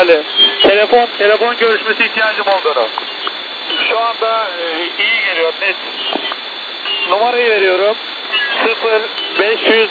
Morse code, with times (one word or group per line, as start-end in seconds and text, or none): none